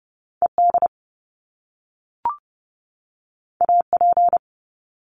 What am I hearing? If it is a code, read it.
Morse code, heard as a keyed tone, about 30 wpm, 700 Hz. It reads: ED AP
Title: Top Two Letter Combinations in Words 30wpm